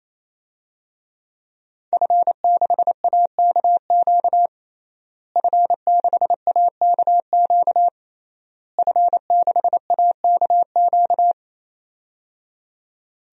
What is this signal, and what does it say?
Morse code, heard as a keyed tone, about 28 wpm, 700 Hz. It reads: F6AKQ F6AKQ F6AKQ
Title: DXCC-100 Call Signs - 5 Characters - Repeated 3x - 28wpm